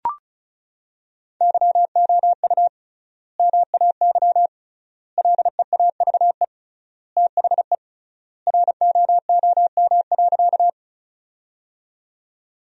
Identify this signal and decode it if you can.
Morse code, heard as a keyed tone, about 34 wpm, 700 Hz. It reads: YOU MAY LEAVE THE ROOM.
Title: 3x Word-Spacing - Sentences from Top 400 Words - 35wpm